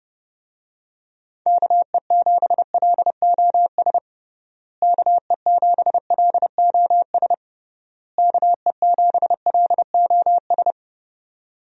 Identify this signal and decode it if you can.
Morse code, heard as a keyed tone, about 30 wpm, 700 Hz. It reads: KE7LOS KE7LOS KE7LOS
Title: USA Call Signs - 6 Characters - Repeated 3x - 30wpm